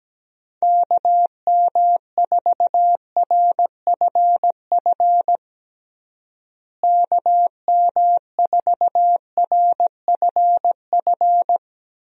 Morse code, heard as a keyed tone, about 17 wpm, 700 Hz. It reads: KM4RFF KM4RFF
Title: USA Call Signs - 6 Characters - Repeated 3x - 17wpm